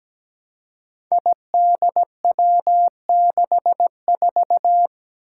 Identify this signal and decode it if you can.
Morse code, heard as a keyed tone, about 17 wpm, 700 Hz. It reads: IDW64
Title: Five Challenging Letter-Number Combinations 17wpm